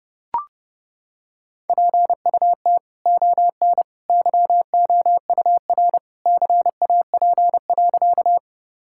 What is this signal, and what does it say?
Morse code, heard as a keyed tone, about 30 wpm, 700 Hz. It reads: PUT ON YOUR CAP.
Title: Sentences from Top 1600 Words 30wpm